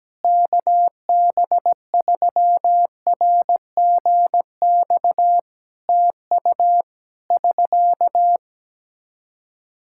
Morse code, heard as a keyed tone, about 17 wpm, 700 Hz. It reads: KB3RGX TU <SK>
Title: Practice QSO - Exchange 4 - Ending - 17wpm